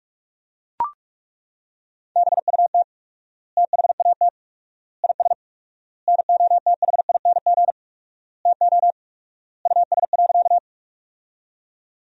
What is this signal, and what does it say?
Morse code, heard as a keyed tone, about 41 wpm, 700 Hz. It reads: BUT THAT IS NOTHING TO US.
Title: 4x Word-Spacing - Sentences from Top 500 Words - 45wpm